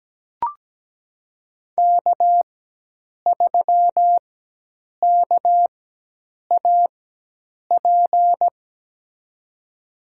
Morse code, heard as a keyed tone, about 17 wpm, spaced as for 8 wpm, 700 Hz. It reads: K3KAP
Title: USA Call Signs - 5 Characters - 4x Character Spacing - 17wpm